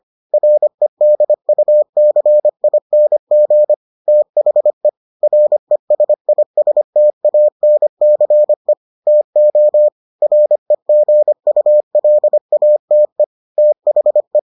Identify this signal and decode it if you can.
Morse code, heard as a keyed tone, about 25 wpm, 600 Hz. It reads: REDUCING THE RESISTANCE TO REGULATE THE